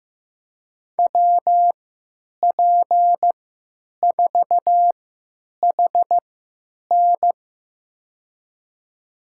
Morse code, heard as a keyed tone, about 15 wpm, spaced as for 9 wpm, 700 Hz. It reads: WP4HN
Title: USA Call Signs - 5 Characters - 3x Character Spacing - 15wpm